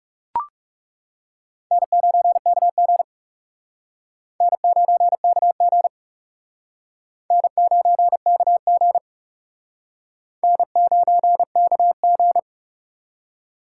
Morse code, heard as a keyed tone, about 36 wpm, 700 Hz. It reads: N9KG N9KG N9KG N9KG
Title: Speed Racing - USA Call Signs 30wpm